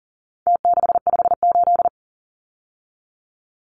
Morse code, heard as a keyed tone, about 40 wpm, 700 Hz. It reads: T658
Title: Numbers - Contest Style - 4-Digit 40wpm